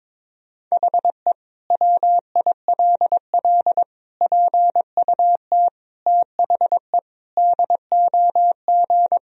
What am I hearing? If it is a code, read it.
Morse code, heard as a keyed tone, about 22 wpm, 700 Hz. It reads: HE WILL PUT THE DOG
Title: Sentences from Top 1000 Words - Review Part 2 - 22wpm